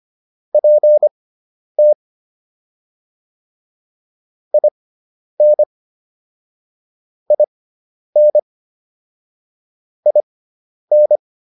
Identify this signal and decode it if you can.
Morse code, heard as a keyed tone, about 25 wpm, spaced as for 10 wpm, 600 Hz. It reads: PT IN IN IN